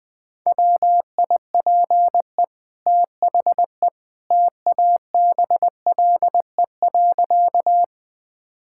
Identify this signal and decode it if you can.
Morse code, heard as a keyed tone, about 20 wpm, 700 Hz. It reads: WIPE THE TABLE.
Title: Easy Sentences with Hints - 20wpm